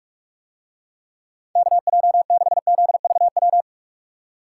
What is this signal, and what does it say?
Morse code, heard as a keyed tone, about 44 wpm, 700 Hz. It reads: KJ6ZVW